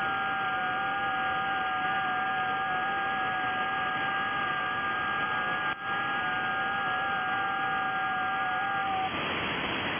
FT8.
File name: IC-7100